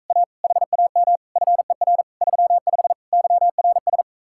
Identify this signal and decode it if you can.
Morse code, heard as a keyed tone, about 45 wpm, 700 Hz. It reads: A HAM FER 35 YRS